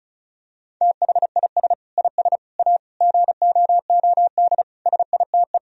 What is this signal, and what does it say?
Morse code, heard as a keyed tone, about 36 wpm, 700 Hz. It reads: THIS IS A GOOD SITE
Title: Sentences from Top 1000 Words - Review Part 4 - 35wpm